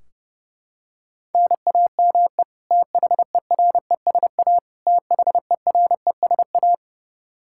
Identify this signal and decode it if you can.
Morse code, heard as a keyed tone, about 30 wpm, 700 Hz. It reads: NAME THERESA THERESA
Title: Practice QSO - Exchange 1 - Beginning - 30wpm